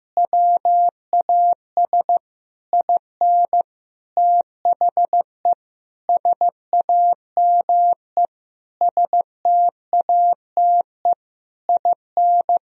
Morse code, heard as a keyed tone, about 15 wpm, 700 Hz. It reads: WAS IN THE SAME STATE IN
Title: Sentences from Top 1000 Words - Review Part 2 - 15wpm